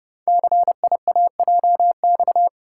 Morse code, heard as a keyed tone, about 30 wpm, 690 Hz. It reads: CIAJX
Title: Code Groups - 5 Letters - 30wpm